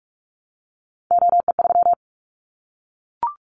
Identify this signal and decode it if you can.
Morse code, heard as a keyed tone, about 44 wpm, 700 Hz. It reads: OE3
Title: Callsign Prefixes for Top 25 DXCC - 45wpm